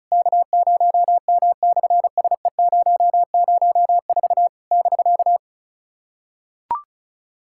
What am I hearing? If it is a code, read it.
Morse code, heard as a keyed tone, about 33 wpm, 700 Hz. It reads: K0M/SE004 <BK>
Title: SOTA QSO Practice 35wpm